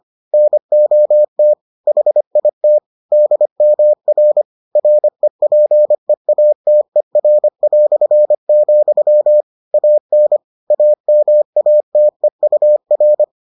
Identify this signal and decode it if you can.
Morse code, heard as a keyed tone, about 25 wpm, 600 Hz. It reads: NOT HIT DMR REPEATER", AN AMATEUR